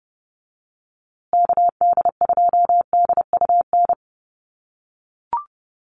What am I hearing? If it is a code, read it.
Morse code, heard as a keyed tone, about 30 wpm, 700 Hz. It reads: KD2DUN
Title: USA Call Signs Repeated 2x 30wpm